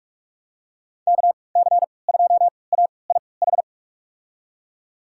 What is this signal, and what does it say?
Morse code, heard as a keyed tone, about 46 wpm, spaced as for 25 wpm, 690 Hz. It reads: KC2AIH
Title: USA Call Signs - 6 Characters - 3x Character Spacing - 45wpm